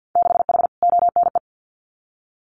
Morse code, heard as a keyed tone, about 50 wpm, 700 Hz. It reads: 6H ONE
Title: Winter Field Day Exchanges Encore 50wpm